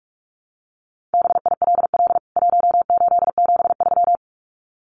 Morse code, heard as a keyed tone, about 44 wpm, 700 Hz. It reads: BILL 1873
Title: CWT Exchange 45wpm